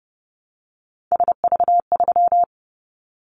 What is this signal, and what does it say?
Morse code, heard as a keyed tone, about 30 wpm, 700 Hz. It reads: SV3
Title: Callsign Prefixes for Top 50 DXCC - 30wpm